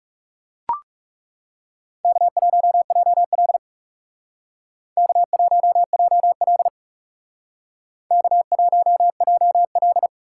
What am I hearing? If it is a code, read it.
Morse code, heard as a keyed tone, about 35 wpm, 700 Hz. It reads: K1JL K1JL K1JL
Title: Speed Racing - USA 1x2 Call Signs 30wpm